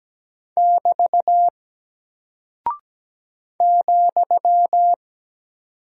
Morse code, heard as a keyed tone, about 17 wpm, 700 Hz. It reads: <BT> ,
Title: Single Character (Letter:Number:Punctuation:Prosign) Alt 17wpm